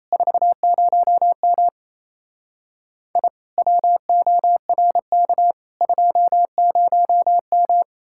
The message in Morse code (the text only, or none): 40M I WORK 20M